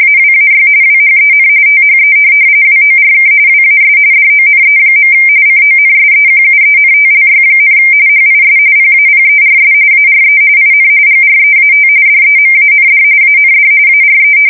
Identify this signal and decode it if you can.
RTTY, about 45.45 baud. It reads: AWAY WHEN THE WORK WAS DONE. CONSEQUENTLY, HIS PAPERS ARE IN GREAT DEMAND AND ARE